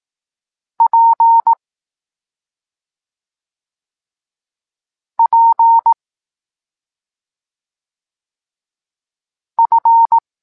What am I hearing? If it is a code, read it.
Morse code, heard as a keyed tone, about 18 wpm, 920 Hz. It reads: P P F
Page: Tekst 25 – Nauka liter P F